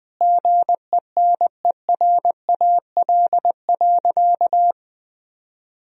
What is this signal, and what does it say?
Morse code, heard as a keyed tone, about 20 wpm, 700 Hz. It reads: GENERAL.
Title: Sentences from Top 1000 Words - Review Part 3 - 20wpm